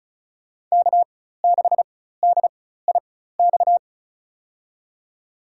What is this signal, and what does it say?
Morse code, heard as a keyed tone, about 36 wpm, spaced as for 16 wpm, 700 Hz. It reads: K6DIX
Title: USA Call Signs - 5 Characters - 4x Character Spacing - 35wpm